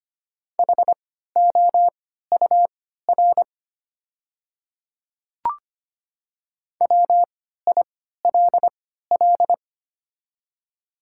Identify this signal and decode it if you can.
Morse code, heard as a keyed tone, about 25 wpm, spaced as for 14 wpm, 690 Hz. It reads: HOUR WILL
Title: 3x Character Spacing - Top 500 Words - 25wpm